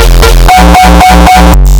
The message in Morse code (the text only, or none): none